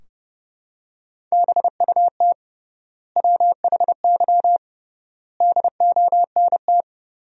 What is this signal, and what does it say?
Morse code, heard as a keyed tone, about 32 wpm, 700 Hz. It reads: BUT WHY DONT
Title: 3x Word-Spacing - Sentences from Top 300 Words - 30wpm